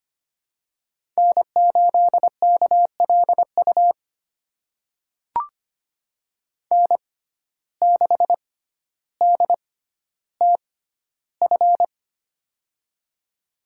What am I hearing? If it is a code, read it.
Morse code, heard as a keyed tone, about 25 wpm, spaced as for 13 wpm, 700 Hz. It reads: N8KLU N6DTF
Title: USA Call Signs - 5 Characters - 6x Character Spacing - 25wpm